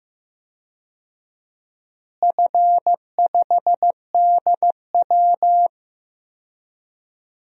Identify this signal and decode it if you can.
Morse code, heard as a keyed tone, about 15 wpm, 700 Hz. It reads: F5DW
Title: DXCC-100 Call Signs - 4 Characters - Repeated 3x - 15wpm